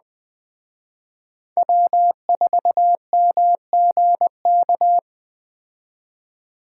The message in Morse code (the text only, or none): W4MGK